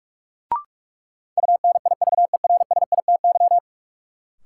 Morse code, heard as a keyed tone, about 46 wpm, 700 Hz. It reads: UNIVERSITY
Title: Sentences from Top 1000 Words - Review Part 4 - New Word Drill - 45wpm